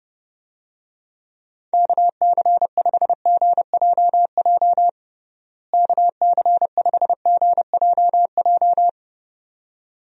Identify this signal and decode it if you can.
Morse code, heard as a keyed tone, about 30 wpm, 700 Hz. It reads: KC5GJJ KC5GJJ
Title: USA Call Signs - 6 Characters - Repeated 2x - 30wpm